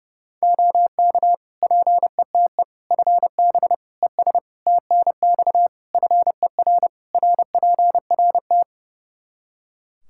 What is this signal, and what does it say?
Morse code, heard as a keyed tone, about 30 wpm, 700 Hz. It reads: OK PETE FB ES TNX FER RPRT
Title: Practice QSO - Exchange 2 - Info - 30wpm